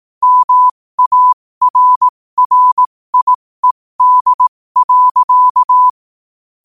Morse code, heard as a keyed tone, about 18 wpm, spaced as for 15 wpm, 1000 Hz. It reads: MARRIED.
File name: qotdfiles / 15wpm